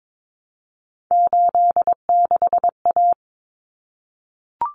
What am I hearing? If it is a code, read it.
Morse code, heard as a keyed tone, about 22 wpm, 700 Hz. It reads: 86A
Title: Common Cut Numbers - 3-Digit 22wpm